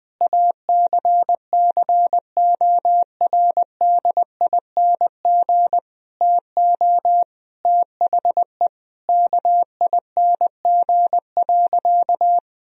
Morse code, heard as a keyed tone, about 20 wpm, 700 Hz. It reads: ACCORDING TO THE KING.
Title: Sentences from Top 1000 Words - Review Part 4 - 20wpm